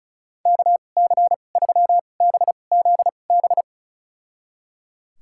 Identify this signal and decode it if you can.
Morse code, heard as a keyed tone, about 34 wpm, spaced as for 25 wpm, 700 Hz. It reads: KC3BZB
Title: USA Call Signs - 6 Characters - 2x Character Spacing - 35wpm